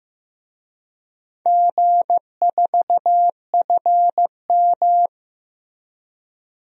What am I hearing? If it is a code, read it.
Morse code, heard as a keyed tone, about 15 wpm, 700 Hz. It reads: G4FM
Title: DXCC-100 Call Signs - 4 Characters - Repeated 2x - 15wpm